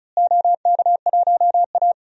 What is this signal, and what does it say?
Morse code, heard as a keyed tone, about 36 wpm, 700 Hz. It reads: OK1A
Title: DXCC-100 Call Signs - 4 Characters - Repeated 2x - 35wpm